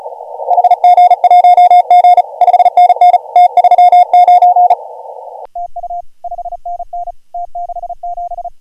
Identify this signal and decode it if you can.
Morse code, heard as a keyed tone, about 38 wpm, 720 Hz, with radio noise. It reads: HG1G 5NN T39 TU 5NN T67
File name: AUDIO2019 / HG1G CW